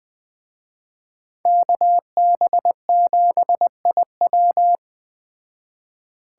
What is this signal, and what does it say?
Morse code, heard as a keyed tone, about 21 wpm, 700 Hz. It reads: KB7IW
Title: Speed Racing - USA 2x2 Call Signs 17wpm